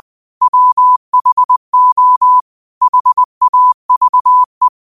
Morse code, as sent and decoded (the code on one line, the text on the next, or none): .-- .... --- / .... .- ...- .
WHO HAVE